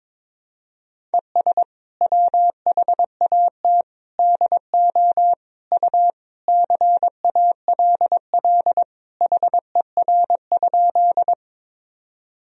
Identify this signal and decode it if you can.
Morse code, heard as a keyed tone, about 22 wpm, 700 Hz. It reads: ES WHAT DO U CALL HER?
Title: Sentences from Top 100 Words and Abbreviations 22wpm